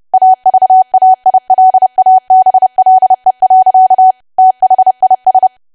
Morse code, heard as a keyed tone, about 30 wpm, 750 Hz. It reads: AVAILABLE. THIS
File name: cwa / W1AW-Level3 / 30WPM